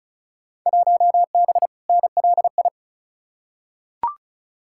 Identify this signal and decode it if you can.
Morse code, heard as a keyed tone, about 36 wpm, 700 Hz. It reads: 1B NLI
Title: Field Day Exchanges Encore - 35wpm